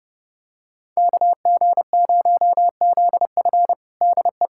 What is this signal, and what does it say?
Morse code, heard as a keyed tone, about 30 wpm, 700 Hz. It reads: KG0ZF DE
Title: USA Callsign DE Callsign - 5 Characters - 30wpm